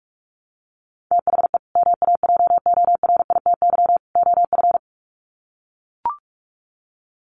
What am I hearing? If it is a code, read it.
Morse code, heard as a keyed tone, about 44 wpm, 700 Hz. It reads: THE MAJORITY OF
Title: Sets of 3 Words 45wpm